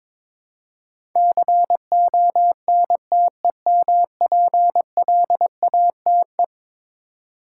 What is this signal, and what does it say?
Morse code, heard as a keyed tone, about 22 wpm, 700 Hz. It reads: CONTEMPLATE
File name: practice / 22wpm / Eleven Letter Words - 22wpm